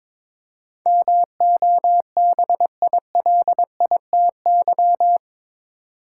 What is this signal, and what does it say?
Morse code, heard as a keyed tone, about 22 wpm, 700 Hz. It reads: MOBILITY
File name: practice / 22wpm / Eight Letter Words - 22wpm